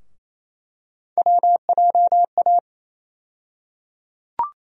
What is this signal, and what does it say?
Morse code, heard as a keyed tone, about 28 wpm, 700 Hz. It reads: WJA
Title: USA Call Sign Postfixes - 3 Letters 28wpm